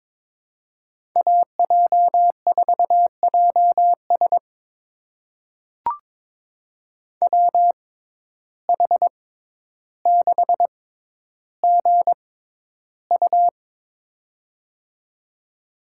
Morse code, heard as a keyed tone, about 22 wpm, spaced as for 11 wpm, 700 Hz. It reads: AJ4JS WH6GU
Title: USA Call Signs - 5 Characters - 6x Character Spacing - 22wpm